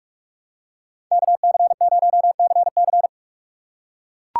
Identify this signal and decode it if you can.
Morse code, heard as a keyed tone, about 44 wpm, 700 Hz. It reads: KC0CC
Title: USA Call Signs - 5 Characters - 2x Character Spacing - 45wpm